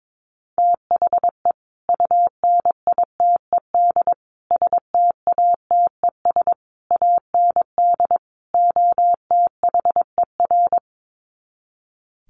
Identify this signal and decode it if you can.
Morse code, heard as a keyed tone, about 22 wpm, 700 Hz. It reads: THE UNITED STATES AND OTHER
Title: Sets of 5 Words Encore 22wpm